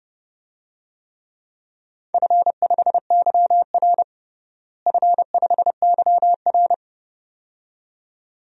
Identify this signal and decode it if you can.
Morse code, heard as a keyed tone, about 31 wpm, 700 Hz. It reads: F5YR F5YR
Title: DXCC-100 Call Signs - 4 Characters - Repeated 2x - 30wpm